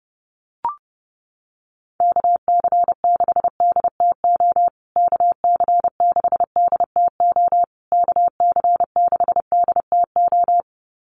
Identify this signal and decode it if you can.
Morse code, heard as a keyed tone, about 28 wpm, 700 Hz. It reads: KC6DTO KC6DTO KC6DTO
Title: USA Call Signs Repeated 3x 30wpm